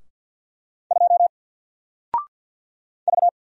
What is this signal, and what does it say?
Morse code, heard as a keyed tone, about 50 wpm, 700 Hz. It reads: 2 V
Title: Single Letter-Number Alt 50wpm